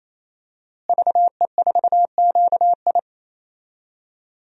Morse code, heard as a keyed tone, about 27 wpm, 700 Hz. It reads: VE4QI
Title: DXCC-100 Call Signs - 5 Characters - Repeated 3x - 28wpm